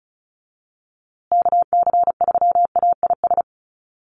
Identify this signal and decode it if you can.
Morse code, heard as a keyed tone, about 34 wpm, 700 Hz. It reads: KC3AIS
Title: USA Call Signs Repeated 3x 35wpm